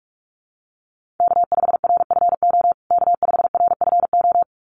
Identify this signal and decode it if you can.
Morse code, heard as a keyed tone, about 44 wpm, 700 Hz. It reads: K5RFO K5RFO
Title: USA Call Signs Repeated 2x 45wpm